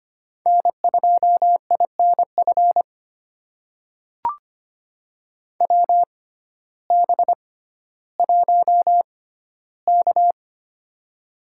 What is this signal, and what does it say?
Morse code, heard as a keyed tone, about 25 wpm, spaced as for 14 wpm, 700 Hz. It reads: N2INF WB1K
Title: USA Call Signs - 6x Character Spacing - 25wpm